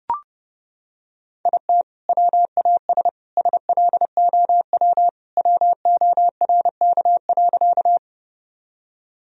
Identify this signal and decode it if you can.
Morse code, heard as a keyed tone, about 29 wpm, 700 Hz. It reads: IT WAS SLOW WORK.
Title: Sentences from Top 1000 Words - Review Part 2 - 30wpm